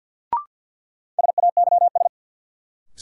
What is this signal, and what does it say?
Morse code, heard as a keyed tone, about 52 wpm, 700 Hz. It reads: SAYS
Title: Sets of 2 Words - New Word Drill - 50wpm